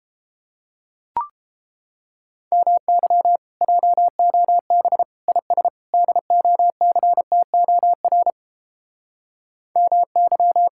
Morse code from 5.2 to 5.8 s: IS

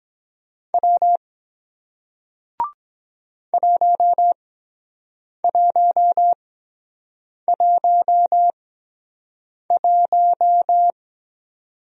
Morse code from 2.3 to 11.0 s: 1111

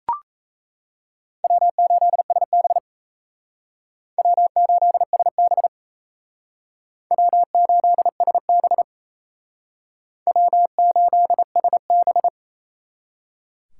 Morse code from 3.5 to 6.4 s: W8SB

